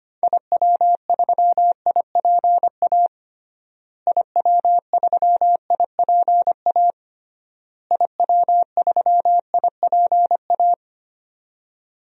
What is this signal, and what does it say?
Morse code, heard as a keyed tone, about 25 wpm, 700 Hz. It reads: IW3IPA IW3IPA IW3IPA
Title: DXCC-100 Call Signs - 6 Characters - Repeated 3x - 25wpm